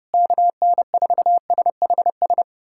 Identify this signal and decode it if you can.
Morse code, heard as a keyed tone, about 30 wpm, 700 Hz. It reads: KN4SHS
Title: USA Callsign DE Callsign 30wpm